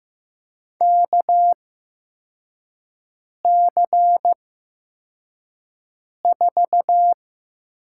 Morse code, heard as a keyed tone, about 15 wpm, spaced as for 4 wpm, 700 Hz. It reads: KC4
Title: USA Call Signs - 8x Character Spacing - 15wpm